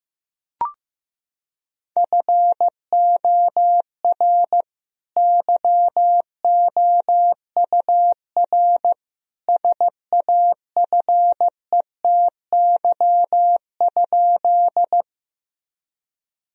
Morse code, heard as a keyed tone, about 15 wpm, 700 Hz. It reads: FOR YOUR SAFETY?
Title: Sentences from Top 1000 Words - Review Part 4 - 15wpm